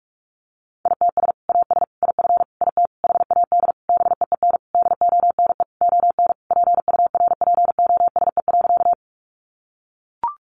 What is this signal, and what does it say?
Morse code, heard as a keyed tone, about 44 wpm, 700 Hz. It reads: ITS AS IF IT HAD BEEN DONE ON PURPOSE.
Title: Sentences from Top 800 Words 45wpm